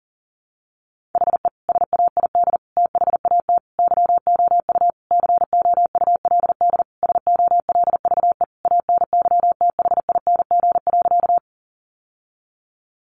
Morse code, heard as a keyed tone, about 40 wpm, 700 Hz. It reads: HE SAID THAT YOU COULD SOLVE ANYTHING.
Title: Sentences from Top 1000 Words 40wpm